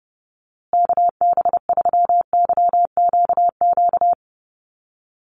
Morse code, heard as a keyed tone, about 30 wpm, 700 Hz. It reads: KB3YQQ
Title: USA Call Signs Repeated 3x 30wpm